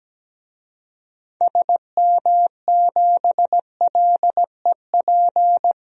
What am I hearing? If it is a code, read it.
Morse code, heard as a keyed tone, about 17 wpm, 700 Hz. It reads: SM7LEP
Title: DXCC-100 Call Signs - 6 Characters - 17wpm